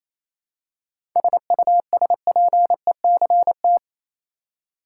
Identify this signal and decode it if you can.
Morse code, heard as a keyed tone, about 28 wpm, 700 Hz. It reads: SUSPECT
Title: Seven Letter Words - 28wpm